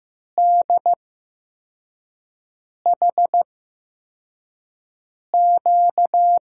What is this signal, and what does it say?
Morse code, heard as a keyed tone, about 15 wpm, spaced as for 4 wpm, 700 Hz. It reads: DHQ
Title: USA Call Signs - 6 Characters - 8x Character Spacing - 15wpm